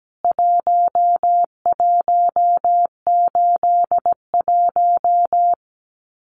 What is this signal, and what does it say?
Morse code, heard as a keyed tone, about 17 wpm, 700 Hz. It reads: 1181
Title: CWT Exchange 17wpm